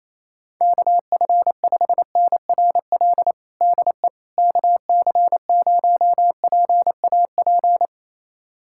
Morse code, heard as a keyed tone, about 28 wpm, 700 Hz. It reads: KF5NRL DE KC0PAP
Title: USA Callsign DE Callsign 28wpm